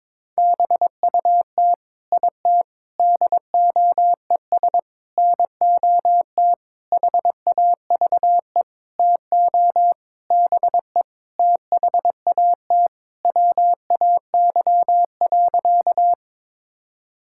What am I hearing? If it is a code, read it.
Morse code, heard as a keyed tone, about 21 wpm, 700 Hz. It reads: BUT IT DOES NOT HAVE TO BE THAT WAY.
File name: practice / 22wpm / Sentences with 8-Word Phrases - 22wpm